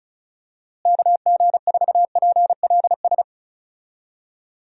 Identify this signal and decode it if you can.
Morse code, heard as a keyed tone, about 36 wpm, 700 Hz. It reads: KG4PLS
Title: USA Call Signs - 6 Characters - Repeated 3x - 35wpm